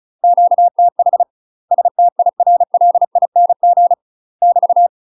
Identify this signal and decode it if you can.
Morse code, heard as a keyed tone, about 34 wpm, 700 Hz. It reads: QTH STIRLING <BT>
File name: UK MASTER QSOs TRAINING / 35WPM / UK MASTER QSOs 0002